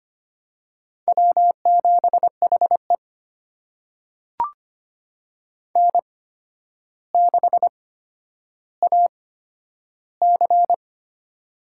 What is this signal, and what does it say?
Morse code, heard as a keyed tone, about 25 wpm, spaced as for 11 wpm, 700 Hz. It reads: W7HE N6AC
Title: USA Call Signs - 4 Characters - 8x Character Spacing - 25wpm